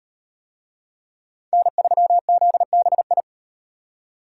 Morse code, heard as a keyed tone, about 38 wpm, 700 Hz. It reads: N3ZBI